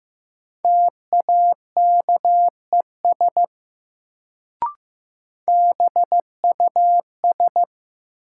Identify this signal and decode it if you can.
Morse code, heard as a keyed tone, about 15 wpm, 700 Hz. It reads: TAKES BUS
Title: Sentences from Top 1000 Words - Review Part 6 - New Word Drill - 15wpm